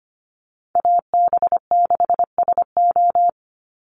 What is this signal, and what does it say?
Morse code, heard as a keyed tone, about 25 wpm, 700 Hz. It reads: AB6SO
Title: NA SOTA Chaser Callsigns 25wpm